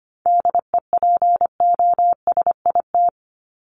Morse code, heard as a keyed tone, about 25 wpm, 700 Hz. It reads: DEPOSIT
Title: Top 1251-1500 Words 25wpm